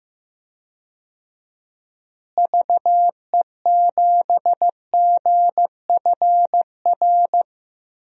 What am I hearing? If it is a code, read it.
Morse code, heard as a keyed tone, about 15 wpm, 700 Hz. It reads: VE7GFR
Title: DXCC-100 Call Signs - 6 Characters - Repeated 3x - 15wpm